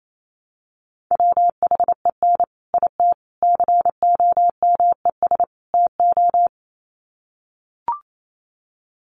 Morse code, heard as a keyed tone, about 28 wpm, 700 Hz. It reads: WHEN IT COMES TO
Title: Sets of 4 Words 28wpm